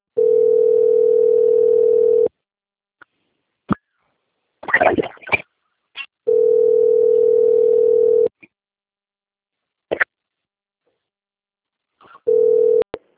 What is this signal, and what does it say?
Ringback tone (North America), 440 + 480 Hz.
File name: out- - - - - .7287